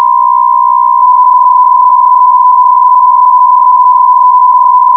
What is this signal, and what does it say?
PSK63, about 62.5 baud.